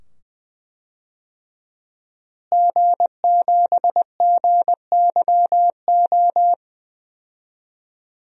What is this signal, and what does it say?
Morse code, heard as a keyed tone, about 21 wpm, 700 Hz. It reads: G7GYO
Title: DXCC-100 Call Signs - 5 Characters - Repeated 2x - 20wpm